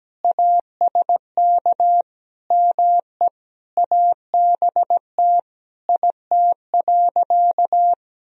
Morse code, heard as a keyed tone, about 17 wpm, 700 Hz. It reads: ASK ME ABT IT.
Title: Sentences from Top 200 Words and Abbreviations 17wpm